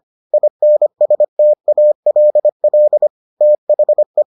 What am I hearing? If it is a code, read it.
Morse code, heard as a keyed tone, about 25 wpm, 600 Hz. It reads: INSTALL THE